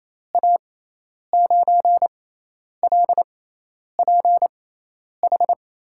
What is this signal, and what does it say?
Morse code, heard as a keyed tone, about 28 wpm, spaced as for 10 wpm, 700 Hz. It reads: A9LPH
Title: USA Call Signs - 6 Characters - 6x Character Spacing - 28wpm